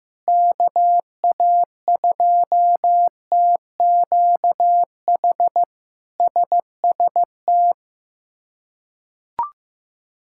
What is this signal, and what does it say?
Morse code, heard as a keyed tone, about 15 wpm, 700 Hz. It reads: KA2TQH SST
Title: K1USN SST QSO Practice - 15wpm